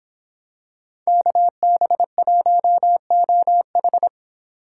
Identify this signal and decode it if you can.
Morse code, heard as a keyed tone, about 26 wpm, 700 Hz. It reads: KB1OH
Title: Speed Racing - USA 2x2 Call Signs 22wpm